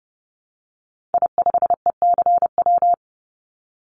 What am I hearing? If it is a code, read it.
Morse code, heard as a keyed tone, about 30 wpm, 700 Hz. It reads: I5ECW
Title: Call Signs - 30wpm